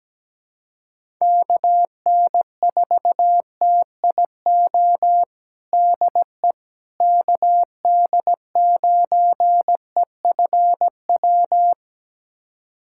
Morse code, heard as a keyed tone, about 17 wpm, 700 Hz. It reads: KN4TIO DE KD9EFW
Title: USA Callsign DE Callsign - 6 Characters - 17wpm